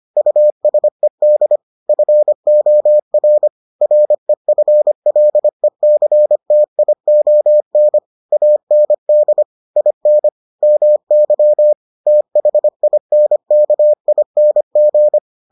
Morse code, heard as a keyed tone, about 25 wpm, 600 Hz. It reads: USED FOR REFLECTION AND IN MY THINKING